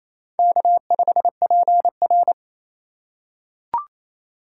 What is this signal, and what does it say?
Morse code, heard as a keyed tone, about 27 wpm, 700 Hz. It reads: K5PR
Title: USA Call Signs - 2x Character Spacing - 28wpm